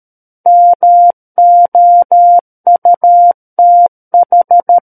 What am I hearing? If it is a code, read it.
Morse code, heard as a keyed tone, about 13 wpm, 700 Hz. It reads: MOUTH
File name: Book One / 04-Chapter 4 - 13 WPM